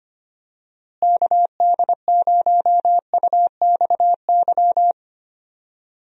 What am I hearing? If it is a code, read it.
Morse code, heard as a keyed tone, about 25 wpm, 700 Hz. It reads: KD0UXY
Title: USA Call Signs - 6 Characters - Repeated 3x - 25wpm